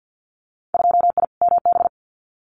Morse code, heard as a keyed tone, about 50 wpm, 700 Hz. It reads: 2I MB
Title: Winter Field Day Exchanges Encore 50wpm